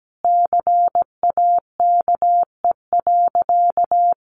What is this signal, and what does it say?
Morse code, heard as a keyed tone, about 17 wpm, 700 Hz. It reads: CAKE.
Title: Additional Easy Phrases 17wpm